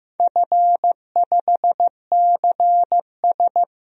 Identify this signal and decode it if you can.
Morse code, heard as a keyed tone, about 16 wpm, 700 Hz. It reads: F5CS
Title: DXCC-100 Call Signs - 4 Characters - Repeated 3x - 15wpm